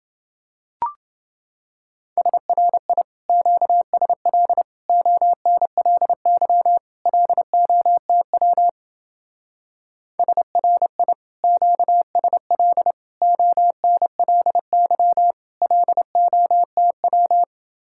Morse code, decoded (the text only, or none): SRI QSL ONLY LOTW SRI QSL ONLY LOTW